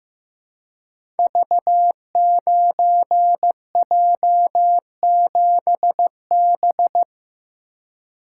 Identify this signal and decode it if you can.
Morse code, heard as a keyed tone, about 16 wpm, 700 Hz. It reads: V9J7B
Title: Code Groups - 5 Letter:Numbers - 15wpm